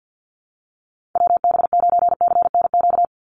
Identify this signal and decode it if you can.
Morse code, heard as a keyed tone, about 50 wpm, 700 Hz. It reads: WB9CNQ
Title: USA Call Signs 50wpm